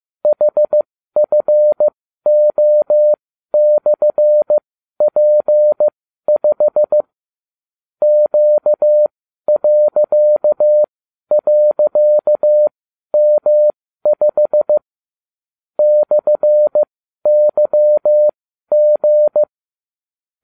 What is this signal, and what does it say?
Morse code, heard as a keyed tone, about 15 wpm, spaced as for 12 wpm, 600 Hz. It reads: HFO/P5 Q..M5 /YG